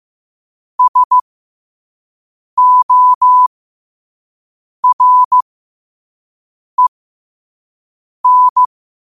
Morse code, heard as a keyed tone, about 15 wpm, spaced as for 5 wpm, 1000 Hz. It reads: SOREN